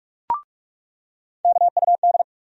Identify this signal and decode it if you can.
Morse code, heard as a keyed tone, about 42 wpm, 700 Hz. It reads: KUD
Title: USA Call Sign Postfixes - 3 Letters 45wpm